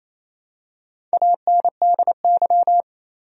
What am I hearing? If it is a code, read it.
Morse code, heard as a keyed tone, about 27 wpm, 700 Hz. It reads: ANDY
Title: USA Common Names - Max 4 Letters - 28wpm